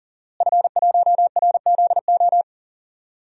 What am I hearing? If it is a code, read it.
Morse code, heard as a keyed tone, about 40 wpm, 700 Hz. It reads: F1RZO
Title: DXCC-100 Call Signs - 5 Characters - Repeated 2x - 40wpm